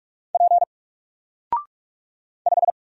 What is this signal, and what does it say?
Morse code, heard as a keyed tone, about 44 wpm, 700 Hz. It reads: P 5
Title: Single Letter-Number Alt 45wpm